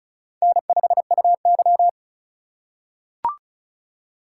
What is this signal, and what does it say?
Morse code, heard as a keyed tone, about 34 wpm, 700 Hz. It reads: N5UY
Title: USA Call Signs - 4x Character Spacing - 35wpm